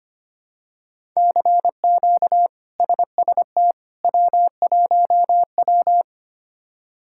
Morse code, heard as a keyed tone, about 25 wpm, 700 Hz. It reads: CQ SST W1W
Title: K1USN SST QSO Practice - 25wpm